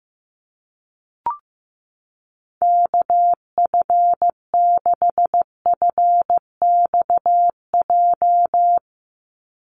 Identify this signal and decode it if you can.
Morse code, heard as a keyed tone, about 15 wpm, 700 Hz. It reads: KF6FXJ
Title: USA Call Signs 15wpm